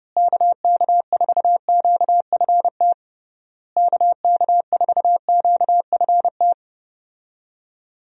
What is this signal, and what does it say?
Morse code, heard as a keyed tone, about 30 wpm, 700 Hz. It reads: KK4QFT KK4QFT
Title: USA Call Signs - 6 Characters - Repeated 3x - 30wpm